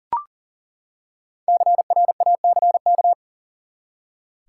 Morse code, heard as a keyed tone, about 40 wpm, 700 Hz. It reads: CRACK
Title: Five Letter Words - 40wpm